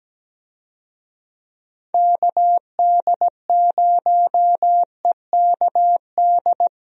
Morse code, heard as a keyed tone, about 17 wpm, 700 Hz. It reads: KD0EKD
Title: USA Call Signs - 6 Characters - Repeated 3x - 17wpm